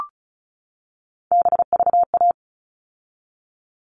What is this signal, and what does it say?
Morse code, heard as a keyed tone, about 34 wpm, 700 Hz. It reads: BVA
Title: All Cut Numbers - 3-Digit 35wpm